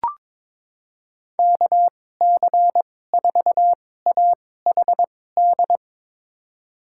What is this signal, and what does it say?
Morse code, heard as a keyed tone, about 21 wpm, spaced as for 16 wpm, 700 Hz. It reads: KC4AHD
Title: USA Call Signs - 6 Characters - 2x Character Spacing - 22wpm